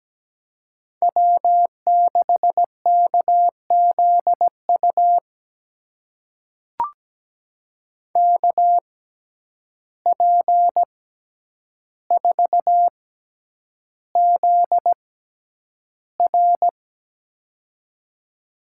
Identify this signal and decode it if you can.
Morse code, heard as a keyed tone, about 17 wpm, spaced as for 9 wpm, 700 Hz. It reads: W6KZU KP4ZR
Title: USA Call Signs - 5 Characters - 6x Character Spacing - 17wpm